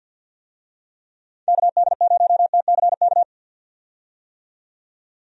Morse code, heard as a keyed tone, about 51 wpm, 700 Hz. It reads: KD0TCK
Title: USA Call Signs - 6 Characters - 50wpm